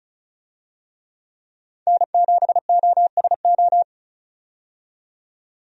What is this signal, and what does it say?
Morse code, heard as a keyed tone, about 36 wpm, 700 Hz. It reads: N7OSO
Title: USA Call Signs - 5 Characters - 35wpm